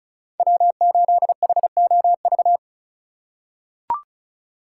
Morse code, heard as a keyed tone, about 36 wpm, 700 Hz. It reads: W8HOV
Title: USA Call Signs - 5 Characters - 4x Character Spacing - 35wpm